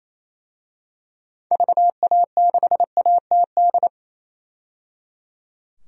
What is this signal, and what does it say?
Morse code, heard as a keyed tone, about 27 wpm, 700 Hz. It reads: VA6ATD
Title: DXCC-100 Call Signs - 6 Characters - 28wpm